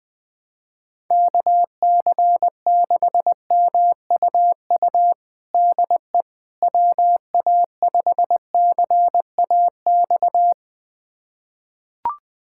Morse code, heard as a keyed tone, about 21 wpm, 700 Hz. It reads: KC6MUU DE WA5CAX
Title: USA Callsign DE Callsign - 6 Characters - 20wpm